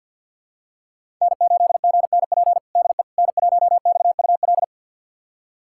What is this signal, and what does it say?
Morse code, heard as a keyed tone, about 51 wpm, 700 Hz. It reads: N8GNP DE N1XUL
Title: USA Callsign DE Callsign - 5 Characters - 50wpm